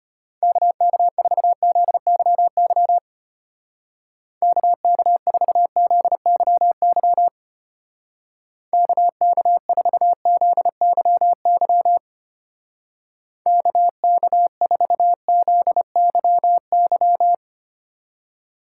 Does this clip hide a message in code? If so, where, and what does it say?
Yes: Morse code from 0.4 to 17.3 s, heard as a keyed tone, about 32 wpm, 700 Hz. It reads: KK4ZYY KK4ZYY KK4ZYY KK4ZYY